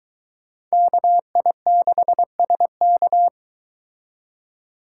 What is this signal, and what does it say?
Morse code, heard as a keyed tone, about 24 wpm, 700 Hz. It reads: KI6SK